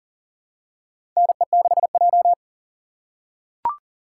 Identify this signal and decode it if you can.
Morse code, heard as a keyed tone, about 40 wpm, 700 Hz. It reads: NE6J
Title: USA Call Signs - 3x Character Spacing - 40wpm